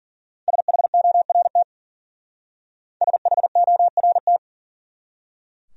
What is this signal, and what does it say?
Morse code, heard as a keyed tone, about 48 wpm, 700 Hz. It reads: SHORT SHORT
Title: Speed-Racing Top 500 Words 40wpm